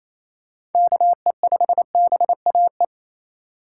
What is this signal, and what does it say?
Morse code, heard as a keyed tone, about 27 wpm, 700 Hz. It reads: KE5BAE
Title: USA Call Signs - 4x Character Spacing - 28wpm